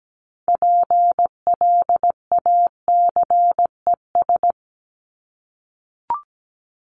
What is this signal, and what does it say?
Morse code, heard as a keyed tone, about 17 wpm, 700 Hz. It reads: PLACES
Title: ES:S Suffix - Top 100 Words with Base Words 17wpm